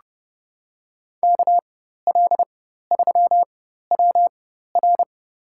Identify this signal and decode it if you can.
Morse code, heard as a keyed tone, about 30 wpm, spaced as for 14 wpm, 700 Hz. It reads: KL3WR
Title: USA Call Signs - 5 Characters - 4x Character Spacing - 30wpm